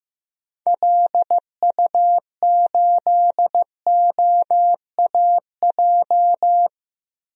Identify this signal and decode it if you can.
Morse code, heard as a keyed tone, about 15 wpm, 700 Hz. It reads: LU8OAJ
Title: DXCC-100 Call Signs - 6 Characters - Repeated 2x - 15wpm